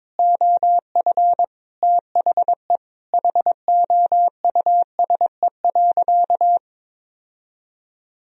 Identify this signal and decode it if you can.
Morse code, heard as a keyed tone, about 22 wpm, 700 Hz. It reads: OF THE HOUSE.
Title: Sentences from Top 1000 Words - Review Part 1 - 22wpm